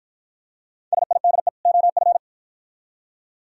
Morse code, heard as a keyed tone, about 49 wpm, 700 Hz. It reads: SIDE OF